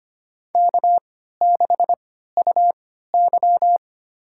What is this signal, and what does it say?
Morse code, heard as a keyed tone, about 25 wpm, spaced as for 14 wpm, 700 Hz. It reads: K6UY
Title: USA Call Signs - 3x Character Spacing - 25wpm